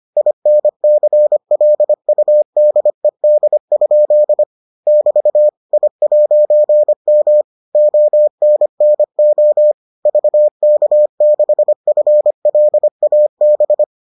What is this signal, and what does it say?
Morse code, heard as a keyed tone, about 25 wpm, 600 Hz. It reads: INCLUDED? <BT> I'M ONNO VK6FLAB